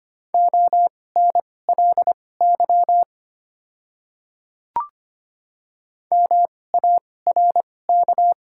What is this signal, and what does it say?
Morse code, heard as a keyed tone, about 25 wpm, spaced as for 18 wpm, 700 Hz. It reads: ONLY MARK
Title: 2x Character Spacing - Top 500 Words - 25wpm